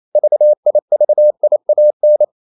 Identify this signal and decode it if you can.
Morse code, heard as a keyed tone, about 28 wpm, 600 Hz. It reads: VIVIAN
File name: CW / Practice / EnglishQuotations0128